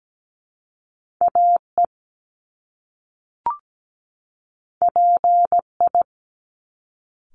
Morse code, heard as a keyed tone, about 17 wpm, 700 Hz. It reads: AE PI
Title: Top Two Letter Combinations in Words 17wpm